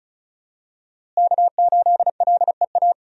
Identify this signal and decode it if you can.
Morse code, heard as a keyed tone, about 34 wpm, 700 Hz. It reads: K8LEA
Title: USA Call Signs - 5 Characters - Repeated 3x - 35wpm